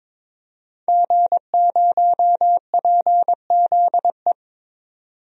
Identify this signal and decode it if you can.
Morse code, heard as a keyed tone, about 22 wpm, 700 Hz. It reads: G0PZE
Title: DXCC-100 Call Signs - 5 Characters - 22wpm